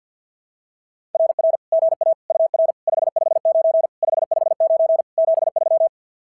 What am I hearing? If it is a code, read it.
Morse code, heard as a keyed tone, about 46 wpm, 660 Hz. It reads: RR GA UR 559 559 73
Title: SOTA QSO Practice 50wpm